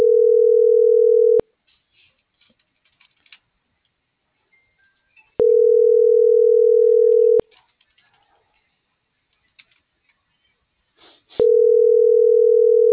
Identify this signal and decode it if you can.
Ringback tone (North America), 440 + 480 Hz.